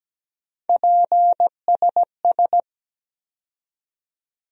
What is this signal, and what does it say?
Morse code, heard as a keyed tone, about 17 wpm, 700 Hz. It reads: PSS
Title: USA Call Sign Postfixes - 3 Letters 17wpm